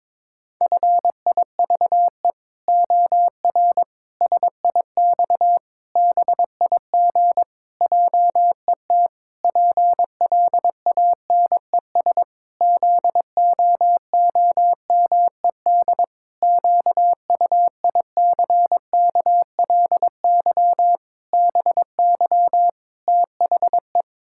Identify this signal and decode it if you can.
Morse code, heard as a keyed tone, about 22 wpm, 700 Hz. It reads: FIVE OR SIX BIG JET PLANES ZOOMED QUICKLY BY THE
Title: Pangram Challenge v2 - 22wpm